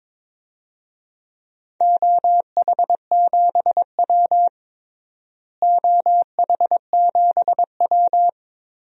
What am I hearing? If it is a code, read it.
Morse code, heard as a keyed tone, about 22 wpm, 700 Hz. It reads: OH7W OH7W
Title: DXCC-100 Call Signs - 4 Characters - Repeated 2x - 22wpm